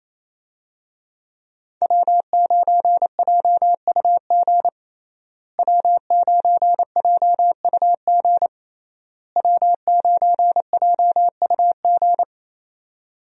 Morse code, heard as a keyed tone, about 28 wpm, 700 Hz. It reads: W9JUG W9JUG W9JUG
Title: USA Call Signs - 5 Characters - Repeated 3x - 28wpm